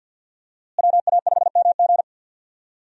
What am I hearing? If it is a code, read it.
Morse code, heard as a keyed tone, about 53 wpm, 700 Hz. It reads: WA5MG
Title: USA Call Signs - 5 Characters - Repeated 2x - 50wpm